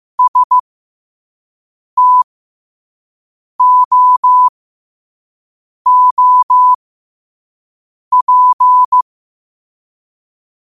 Morse code, heard as a keyed tone, about 15 wpm, spaced as for 5 wpm, 1000 Hz. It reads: STOOP
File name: qotdfiles / 5wpm